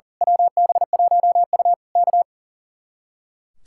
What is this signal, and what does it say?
Morse code, heard as a keyed tone, about 38 wpm, 700 Hz. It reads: WB1U K
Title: Speed-Racing Common QSO Phrases Encore - 40wpm